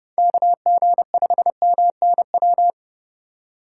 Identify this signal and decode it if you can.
Morse code, heard as a keyed tone, about 30 wpm, 700 Hz. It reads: KG5MNW
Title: USA Call Signs - 6 Characters - 4x Character Spacing - 30wpm